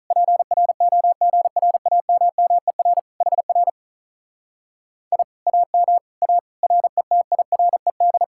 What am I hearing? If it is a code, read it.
Morse code, heard as a keyed tone, about 46 wpm, 700 Hz. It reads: PROGRAMMER HR I AM A RETIRED